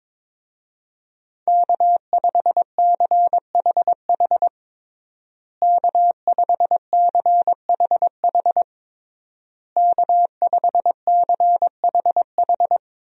Morse code, heard as a keyed tone, about 22 wpm, 700 Hz. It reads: K5CHH K5CHH K5CHH
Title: USA Call Signs - 5 Characters - Repeated 3x - 22wpm